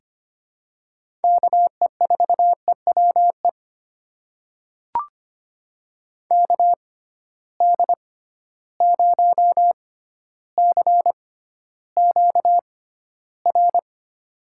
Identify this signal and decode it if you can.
Morse code, heard as a keyed tone, about 25 wpm, spaced as for 13 wpm, 700 Hz. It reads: KE4EWE KD0CQR
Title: USA Call Signs - 6 Characters - 6x Character Spacing - 25wpm